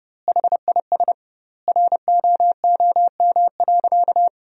Morse code, heard as a keyed tone, about 32 wpm, 700 Hz. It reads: HIS ROOM.
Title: 2x Word-Spacing - Sentences from Top 300 Words - 30wpm